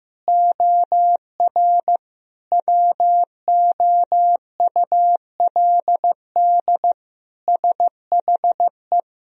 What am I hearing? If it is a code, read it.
Morse code, heard as a keyed tone, about 16 wpm, 700 Hz. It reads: OR WOULD SHE
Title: Sentences from Top 1000 Words - Review Part 1 - 15wpm